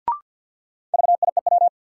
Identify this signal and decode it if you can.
Morse code, heard as a keyed tone, about 52 wpm, 700 Hz. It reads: VIEW
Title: Sentences from Top 900 Words - New Word Drill - 50wpm